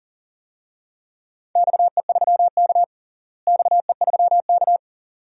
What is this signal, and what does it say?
Morse code, heard as a keyed tone, about 40 wpm, 700 Hz. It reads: XE3K XE3K
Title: DXCC-100 Call Signs - 4 Characters - Repeated 2x - 40wpm